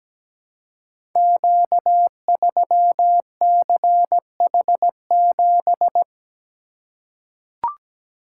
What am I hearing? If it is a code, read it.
Morse code, heard as a keyed tone, about 17 wpm, 700 Hz. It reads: Q3CH7
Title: Code Groups - 5 Letter:Numbers - 17wpm